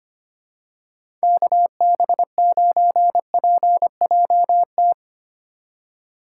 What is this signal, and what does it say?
Morse code, heard as a keyed tone, about 25 wpm, 700 Hz. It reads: KB9PJT
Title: USA Call Signs - 6 Characters - 4x Character Spacing - 25wpm